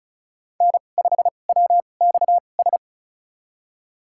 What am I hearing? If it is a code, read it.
Morse code, heard as a keyed tone, about 36 wpm, spaced as for 25 wpm, 700 Hz. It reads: N5WXS
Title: USA Call Signs - 5 Characters - 2x Character Spacing - 35wpm